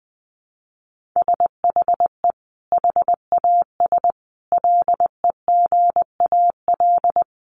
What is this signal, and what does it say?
Morse code, heard as a keyed tone, about 20 wpm, 700 Hz. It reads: SHE HAS LEGAL
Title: Sentences from Top 1200 Words 20wpm